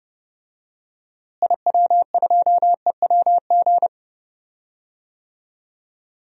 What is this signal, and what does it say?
Morse code, heard as a keyed tone, about 30 wpm, 700 Hz. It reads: IW2EWG
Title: DXCC-100 Call Signs - 6 Characters - Repeated 3x - 30wpm